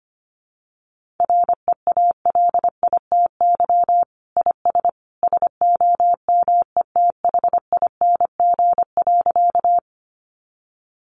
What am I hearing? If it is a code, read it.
Morse code, heard as a keyed tone, about 25 wpm, 700 Hz. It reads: REALITY IS SOMETHING.
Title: Sentences from Top 1200 Words 25wpm